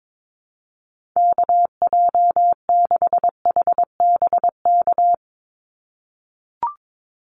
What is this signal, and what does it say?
Morse code, heard as a keyed tone, about 22 wpm, 700 Hz. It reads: KJ6HBK
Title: USA Call Signs 22wpm